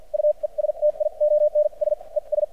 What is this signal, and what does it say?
Morse code, heard as a keyed tone, about 56 wpm, spaced as for 42 wpm, 600 Hz. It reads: UESTIONS ES
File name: archived / CRSnet